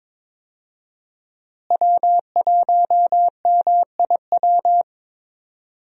Morse code, heard as a keyed tone, about 22 wpm, 700 Hz. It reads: W1MIW
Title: USA Call Signs - 5 Characters - Repeated 3x - 22wpm